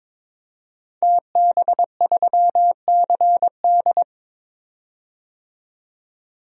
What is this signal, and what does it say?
Morse code, heard as a keyed tone, about 22 wpm, 700 Hz. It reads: TB3CD
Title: DXCC-100 Call Signs - 5 Characters - 22wpm